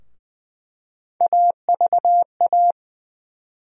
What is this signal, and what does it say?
Morse code, heard as a keyed tone, about 20 wpm, 700 Hz. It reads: AVA
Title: USA Call Sign Postfixes - 3 Letters 20wpm